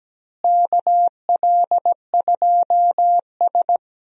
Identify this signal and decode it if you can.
Morse code, heard as a keyed tone, about 17 wpm, 700 Hz. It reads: KL2S
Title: USA Call Signs - 4 Characters - Repeated 2x - 17wpm